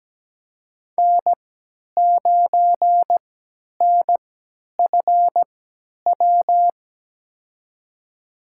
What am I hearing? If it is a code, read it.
Morse code, heard as a keyed tone, about 17 wpm, spaced as for 10 wpm, 700 Hz. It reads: N9NFW
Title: USA Call Signs - 5 Characters - 3x Character Spacing - 17wpm